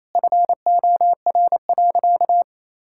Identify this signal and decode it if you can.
Morse code, heard as a keyed tone, about 28 wpm, 700 Hz. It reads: FOR.
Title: Sentences from Top 1600 Words 28wpm